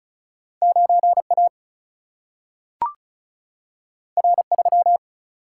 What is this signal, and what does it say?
Morse code, heard as a keyed tone, about 36 wpm, 700 Hz. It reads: 9A R3
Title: Single-Letter Single-Number 35wpm